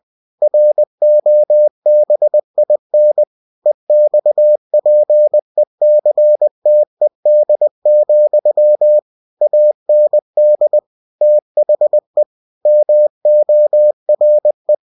Morse code, heard as a keyed tone, about 20 wpm, 600 Hz. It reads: ROBIN EXPECTED, AND THE MORE
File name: cw-books / winnie-the-pooh / winnie-20wpm-page-0099